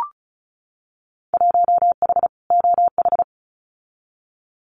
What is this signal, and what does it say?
Morse code, heard as a keyed tone, about 36 wpm, 690 Hz. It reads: 1H OH
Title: Winter Field Day Exchanges 35wpm